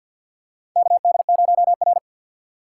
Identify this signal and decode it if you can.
Morse code, heard as a keyed tone, about 51 wpm, 700 Hz. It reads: KD0R
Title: USA Call Signs - 4 Characters - 2x Character Spacing - 50wpm